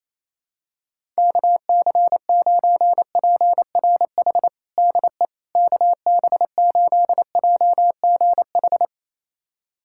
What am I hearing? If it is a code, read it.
Morse code, heard as a keyed tone, about 27 wpm, 700 Hz. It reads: KC9PRH DE KB8JGH
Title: USA Callsign DE Callsign - 6 Characters - 28wpm